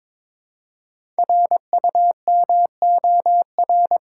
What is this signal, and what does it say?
Morse code, heard as a keyed tone, about 22 wpm, 700 Hz. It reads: RUMOR
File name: 5 Letter Words - First 3 Letters Given - 22wpm